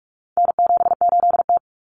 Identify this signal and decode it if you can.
Morse code, heard as a keyed tone, about 44 wpm, 700 Hz. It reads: N78T
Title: Common Cut Numbers - 4-Digit 45wpm